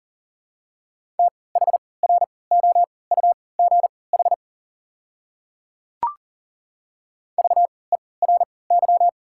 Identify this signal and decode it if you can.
Morse code, heard as a keyed tone, about 40 wpm, spaced as for 23 wpm, 690 Hz. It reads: THROUGH VERY
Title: 3x Character Spacing - Top 200 Words - 40wpm